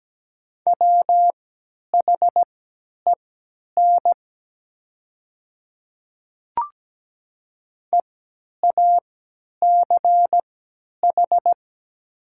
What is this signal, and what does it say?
Morse code, heard as a keyed tone, about 17 wpm, spaced as for 10 wpm, 700 Hz. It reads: WHEN EACH
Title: 3x Character Spacing - Top 200 Words - 17wpm